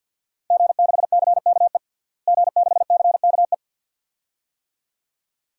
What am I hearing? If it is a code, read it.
Morse code, heard as a keyed tone, about 51 wpm, 700 Hz. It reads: G6CKE G6CKE
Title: DXCC-100 Call Signs - 5 Characters - Repeated 2x - 50wpm